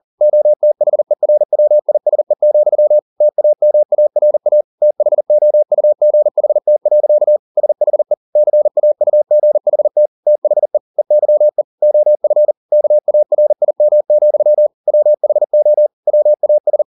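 Morse code, heard as a keyed tone, about 40 wpm, 600 Hz. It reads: OTHERWISE, TAMARA THOUGHT. SHE CAUGHT THE EYE OF KARIM, WHO WAS
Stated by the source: Morse code at 40 wpm, practice text, CW med Gnister.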